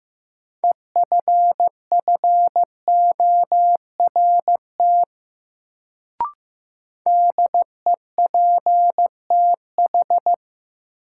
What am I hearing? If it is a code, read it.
Morse code, heard as a keyed tone, about 15 wpm, 700 Hz. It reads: EFFORT DEPTH
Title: Sentences from Top 1200 Words - New Word Drill - 15wpm